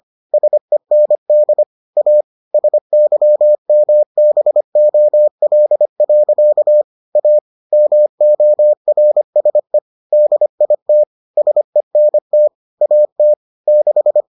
Morse code, heard as a keyed tone, about 25 wpm, 600 Hz. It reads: SEND A SYMBOL. A MORSE DIT SENT AT 6